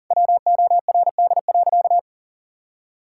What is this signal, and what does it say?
Morse code, heard as a keyed tone, about 38 wpm, 700 Hz. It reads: WORD.
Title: Sentences from Top 1000 Words - Review Part 6 - 40wpm